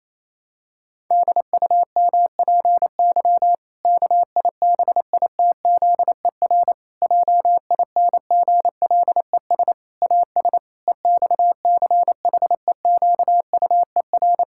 Morse code, heard as a keyed tone, about 28 wpm, 700 Hz. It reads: DUMPY KIBITZER JINGLES AS EXCHEQUER
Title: Pangram Challenge v2 - 28wpm